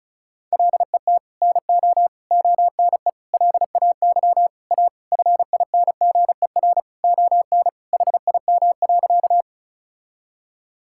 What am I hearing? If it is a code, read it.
Morse code, heard as a keyed tone, about 36 wpm, 700 Hz. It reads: LET NO ONE LAY A FINGER ON HIM.
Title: Sentences from Top 1600 Words 35wpm